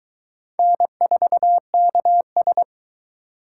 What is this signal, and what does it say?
Morse code, heard as a keyed tone, about 23 wpm, 700 Hz. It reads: N4KS